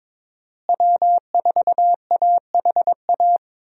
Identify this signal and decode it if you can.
Morse code, heard as a keyed tone, about 22 wpm, 700 Hz. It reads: W4AHA
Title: USA Call Signs - 5 Characters - Repeated 3x - 22wpm